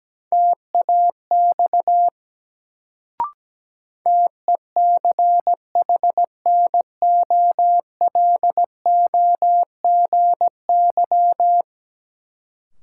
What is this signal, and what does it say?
Morse code, heard as a keyed tone, about 17 wpm, 700 Hz. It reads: TAX TECHNOLOGY
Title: Sentences from Top 1000 Words - Review Part 5 - New Word Drill - 17wpm